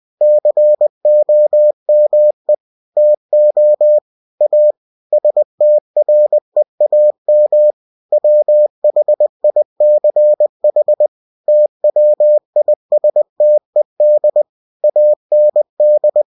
Morse code, heard as a keyed tone, about 20 wpm, 600 Hz. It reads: COME TO A STREAM WHICH TWISTED AND
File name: cw-books / winnie-the-pooh / winnie-20wpm-page-0082